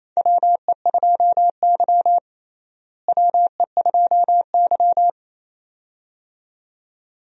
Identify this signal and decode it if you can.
Morse code, heard as a keyed tone, about 28 wpm, 700 Hz. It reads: WE2Y WE2Y
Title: USA Call Signs - 4 Characters - Repeated 2x - 28wpm